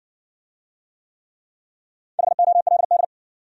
Morse code, heard as a keyed tone, about 55 wpm, 700 Hz. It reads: HOLD